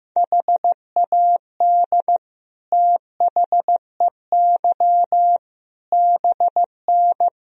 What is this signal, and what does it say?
Morse code, heard as a keyed tone, about 15 wpm, 700 Hz. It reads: HAD THEY BN
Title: Sentences from Top 400 Words and Abbreviations 15wpm